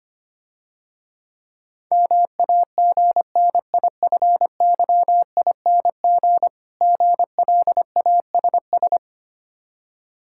Morse code, heard as a keyed tone, about 26 wpm, 700 Hz. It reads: MAGNIFYING GLASS
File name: practice / 25wpm / Difficult Words - Speed-Racing with Hints - 25wpm